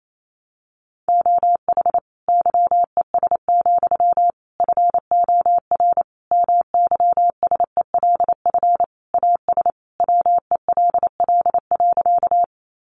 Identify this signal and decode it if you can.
Morse code, heard as a keyed tone, about 28 wpm, 700 Hz. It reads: OH YES, FOR MYSELF AS WELL.
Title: Sentences from Top 1300 Words 28wpm